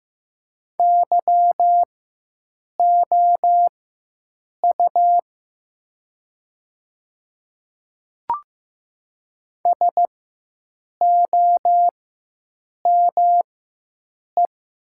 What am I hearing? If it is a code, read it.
Morse code, heard as a keyed tone, about 15 wpm, spaced as for 7 wpm, 700 Hz. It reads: YOU SOME
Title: 4x Character Spacing - Top 100 Words - 15wpm